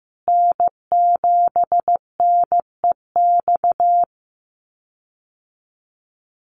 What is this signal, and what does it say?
Morse code, heard as a keyed tone, about 15 wpm, 700 Hz. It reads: N7NEX
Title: USA Call Signs Repeated 2x 15wpm